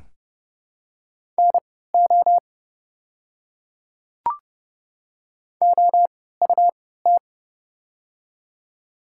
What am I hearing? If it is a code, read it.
Morse code, heard as a keyed tone, about 30 wpm, spaced as for 17 wpm, 700 Hz. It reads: NO OUT
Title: 3x Character Spacing - Top 100 Words - 30wpm